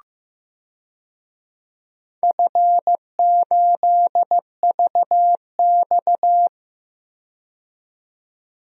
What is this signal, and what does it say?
Morse code, heard as a keyed tone, about 15 wpm, 700 Hz. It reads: F8VX
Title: DXCC-100 Call Signs - 4 Characters - 15wpm